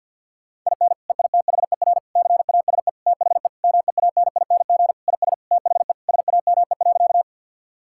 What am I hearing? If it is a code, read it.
Morse code, heard as a keyed tone, about 50 wpm, 700 Hz. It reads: IN EITHER CASE THE MEANING IS THE SAME.
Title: Sentences from Top 1000 Words - Review Part 6 - 50wpm